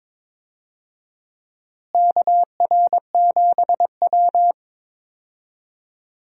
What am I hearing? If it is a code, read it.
Morse code, heard as a keyed tone, about 22 wpm, 700 Hz. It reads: KR7W
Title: USA Call Signs - 4 Characters - Repeated 3x - 22wpm